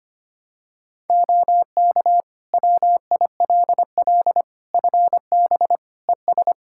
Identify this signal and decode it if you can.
Morse code, heard as a keyed tone, about 25 wpm, 700 Hz. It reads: OK WILL FB ES
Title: Practice QSO - Exchange 2 - Info - 25wpm